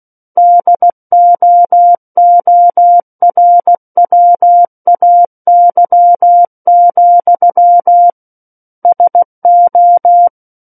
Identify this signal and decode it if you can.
Morse code, heard as a keyed tone, about 16 wpm, 700 Hz. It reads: DOORWAY, SO
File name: Book One / 07-Chapter 7 - 16 WPM